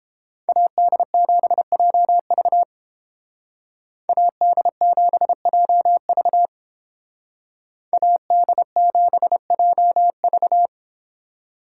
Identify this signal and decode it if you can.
Morse code, heard as a keyed tone, about 31 wpm, 700 Hz. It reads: AD7JV AD7JV AD7JV